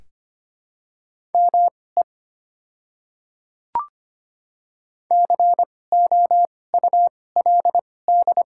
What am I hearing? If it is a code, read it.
Morse code, heard as a keyed tone, about 25 wpm, spaced as for 18 wpm, 700 Hz. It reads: ME COULD
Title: 2x Character Spacing - Top 200 Words - 25wpm